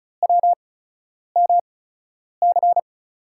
Morse code, heard as a keyed tone, about 36 wpm, spaced as for 10 wpm, 700 Hz. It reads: WMC
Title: USA Call Signs - 5 Characters - 8x Character Spacing - 35wpm